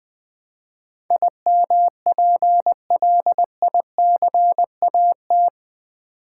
Morse code, heard as a keyed tone, about 20 wpm, 700 Hz. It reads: IMPLICAT
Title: Build A Word - 4 Letters At A Time - 11-Letter Words - 20wpm